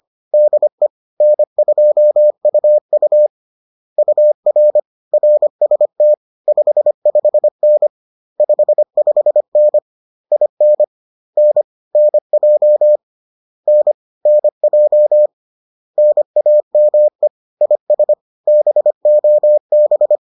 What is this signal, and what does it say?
Morse code, heard as a keyed tone, about 25 wpm, 600 Hz. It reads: DE N2UU UR RST 55N 55N IN N NJ N NJ NAME IS BOB